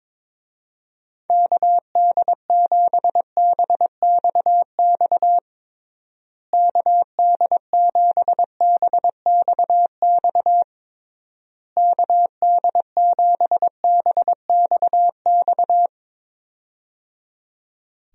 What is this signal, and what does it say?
Morse code, heard as a keyed tone, about 22 wpm, 700 Hz. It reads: KD7BXX KD7BXX KD7BXX
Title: USA Call Signs - 6 Characters - Repeated 3x - 22wpm